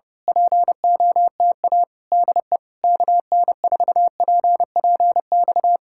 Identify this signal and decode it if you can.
Morse code, heard as a keyed tone, about 29 wpm, 700 Hz. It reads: POTA DE KN4PPX
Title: Speed-Racing Common QSO Phrases Encore - 25wpm